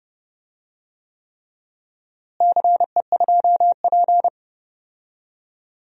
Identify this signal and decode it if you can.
Morse code, heard as a keyed tone, about 32 wpm, 700 Hz. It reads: CE2P
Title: DXCC-100 Call Signs - 4 Characters - Repeated 3x - 30wpm